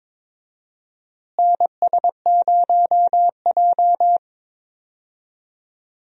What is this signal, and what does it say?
Morse code, heard as a keyed tone, about 22 wpm, 700 Hz. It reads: NS0J
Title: USA Call Signs - 4 Characters - Repeated 2x - 22wpm